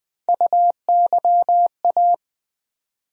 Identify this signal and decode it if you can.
Morse code, heard as a keyed tone, about 20 wpm, 700 Hz. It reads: UYA
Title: USA Call Sign Postfixes - 3 Letters 20wpm